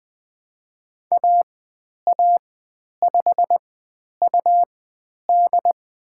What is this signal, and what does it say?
Morse code, heard as a keyed tone, about 20 wpm, spaced as for 10 wpm, 700 Hz. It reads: AA5UD
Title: 102 Lesson 17g - Letter u - Call Signs 20wpm